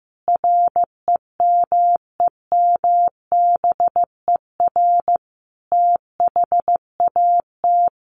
Morse code, heard as a keyed tone, about 15 wpm, 700 Hz. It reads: REMEMBER THAT
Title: Sentences from Top 1100 Words 15wpm